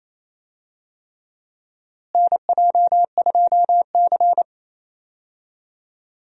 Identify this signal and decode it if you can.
Morse code, heard as a keyed tone, about 28 wpm, 700 Hz. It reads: NJ2C
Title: USA Call Signs - 4 Characters - Repeated 3x - 28wpm